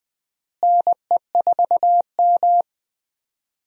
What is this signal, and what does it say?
Morse code, heard as a keyed tone, about 20 wpm, 700 Hz. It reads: NE4M
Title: USA Call Signs - 4 Characters - Repeated 3x - 20wpm